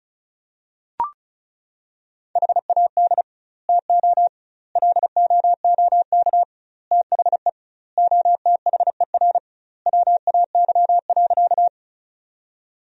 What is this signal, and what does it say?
Morse code, heard as a keyed tone, about 36 wpm, 700 Hz. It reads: HAD TO LOOK THE OTHER WAY.
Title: 2x Word-Spacing - Sentences from Top 100 Words - 35wpm